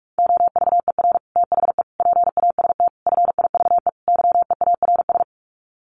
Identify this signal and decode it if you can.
Morse code, heard as a keyed tone, about 44 wpm, 700 Hz. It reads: OVER THE PAST FIVE YEARS
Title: Sets of 5 Words Encore 45wpm